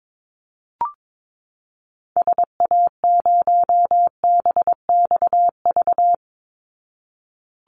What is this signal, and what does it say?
Morse code, heard as a keyed tone, about 22 wpm, 700 Hz. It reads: SA0BXV
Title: Call Signs Repeated 3x - 22wpm